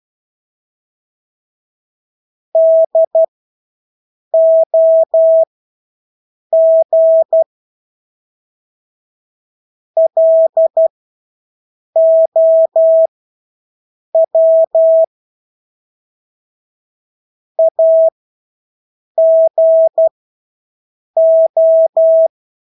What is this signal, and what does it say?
Morse code, heard as a keyed tone, about 12 wpm, spaced as for 6 wpm, 650 Hz. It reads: DOG LOW AGO
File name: Engelske ord 12-6 wpm